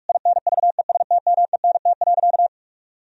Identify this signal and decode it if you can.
Morse code, heard as a keyed tone, about 41 wpm, 700 Hz. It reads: INVESTMENT.
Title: Sentences from Top 1000 Words - Review Part 4 - 45wpm